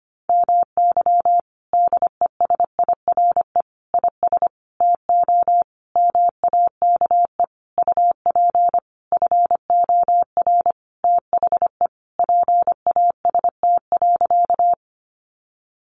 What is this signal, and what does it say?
Morse code, heard as a keyed tone, about 25 wpm, 700 Hz. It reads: MY DESIRE IS TO MAKE UP FOR THE PAST.
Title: Sentences from Top 1100 Words 25wpm